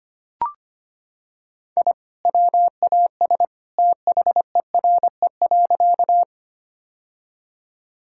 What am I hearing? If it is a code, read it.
Morse code, heard as a keyed tone, about 25 wpm, 700 Hz. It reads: I WAS THERE.
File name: practice / 25wpm / Sentences with 2-Word Phrases - 25wpm